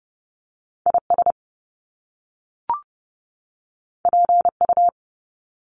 Morse code, heard as a keyed tone, about 30 wpm, 700 Hz. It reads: IS PU
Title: Top Two Letter Combinations in Words 30wpm